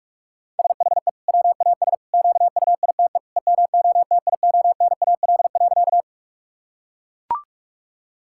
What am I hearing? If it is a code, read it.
Morse code, heard as a keyed tone, about 42 wpm, 700 Hz. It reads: SHE WAS QUITE EMOTIONAL.
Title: Sentences from Top 1000 Words - Review Part 5 - 45wpm